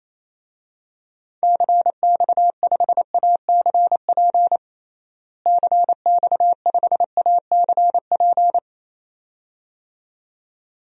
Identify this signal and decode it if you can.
Morse code, heard as a keyed tone, about 28 wpm, 700 Hz. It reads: CX5ACP CX5ACP
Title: DXCC-100 Call Signs - 6 Characters - Repeated 2x - 28wpm